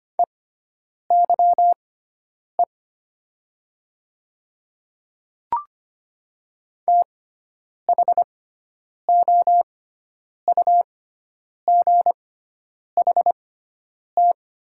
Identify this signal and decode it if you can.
Morse code, heard as a keyed tone, about 25 wpm, spaced as for 9 wpm, 700 Hz. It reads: EYE THOUGHT
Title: 6x Character Spacing - Top 300 Words - 25wpm